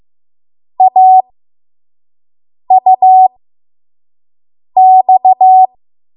Morse code, heard as a keyed tone, about 15 wpm, spaced as for 5 wpm, 750 Hz. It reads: AUX
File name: Morse / 05WPM